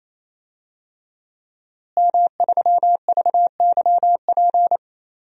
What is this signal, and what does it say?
Morse code, heard as a keyed tone, about 27 wpm, 700 Hz. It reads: M3VYP
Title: DXCC-100 Call Signs - 5 Characters - Repeated 3x - 28wpm